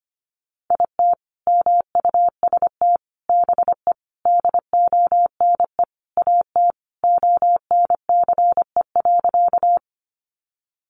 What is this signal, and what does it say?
Morse code, heard as a keyed tone, about 25 wpm, 700 Hz. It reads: IT MUST BE DONE AT ONCE.
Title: Sentences from Top 500 Words 25wpm